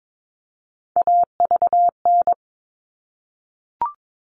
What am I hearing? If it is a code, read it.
Morse code, heard as a keyed tone, about 22 wpm, 700 Hz. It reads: AVN
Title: All Cut Numbers - 3-Digit 22wpm